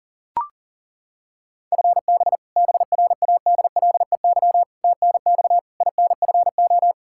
Morse code, heard as a keyed tone, about 37 wpm, 700 Hz. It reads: FB BRADLEY TNX INFO
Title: Practice QSO - Exchange 3 - More Info - 40wpm